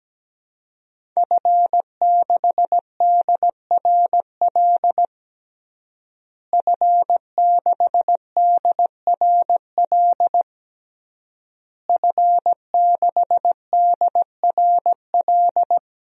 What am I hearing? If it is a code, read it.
Morse code, heard as a keyed tone, about 17 wpm, 700 Hz. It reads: F6DRL F6DRL F6DRL
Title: DXCC-100 Call Signs - 5 Characters - Repeated 3x - 17wpm